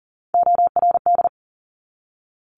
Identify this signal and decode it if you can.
Morse code, heard as a keyed tone, about 42 wpm, 700 Hz. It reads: ORD
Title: Top Three Letter Combinations in Words 40wpm